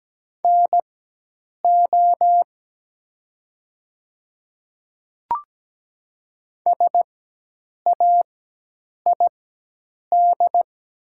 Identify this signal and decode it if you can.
Morse code, heard as a keyed tone, about 17 wpm, spaced as for 8 wpm, 700 Hz. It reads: NO SAID
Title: 4x Character Spacing - Top 500 Words - 17wpm